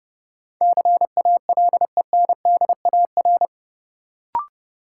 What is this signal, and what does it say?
Morse code, heard as a keyed tone, about 30 wpm, 700 Hz. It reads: CALENDAR
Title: Sentences from Top 1600 Words - New Word Drill - 30wpm